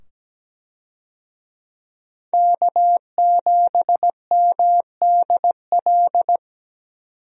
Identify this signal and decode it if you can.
Morse code, heard as a keyed tone, about 17 wpm, 700 Hz. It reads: K7MDL
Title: USA Call Signs - 5 Characters - 17wpm